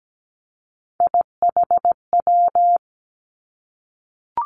Morse code, heard as a keyed tone, about 17 wpm, 700 Hz. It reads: IHW
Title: Three Challenging Letter-Number Combinations 17wpm